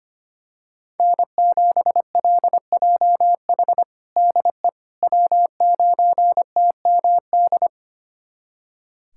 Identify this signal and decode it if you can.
Morse code, heard as a keyed tone, about 25 wpm, 700 Hz. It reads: N7LJH DE W9TMD
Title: USA Callsign DE Callsign - 5 Characters - 25wpm